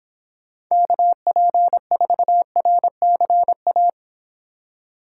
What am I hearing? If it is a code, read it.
Morse code, heard as a keyed tone, about 25 wpm, 700 Hz. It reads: KP4RCA